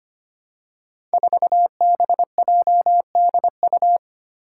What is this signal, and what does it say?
Morse code, heard as a keyed tone, about 25 wpm, 700 Hz. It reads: 4BJDU
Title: Five Challenging Letter-Number Combinations 25wpm